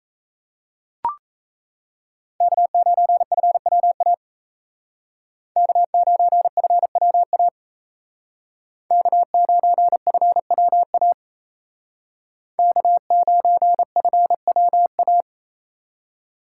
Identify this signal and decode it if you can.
Morse code, heard as a keyed tone, about 36 wpm, 700 Hz. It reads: K9FWA K9FWA K9FWA K9FWA